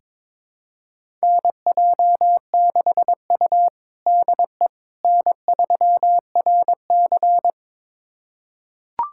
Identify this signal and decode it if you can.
Morse code, heard as a keyed tone, about 21 wpm, 700 Hz. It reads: NJ6U DE N3RC
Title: USA Callsign DE Callsign - 4 Characters - 22wpm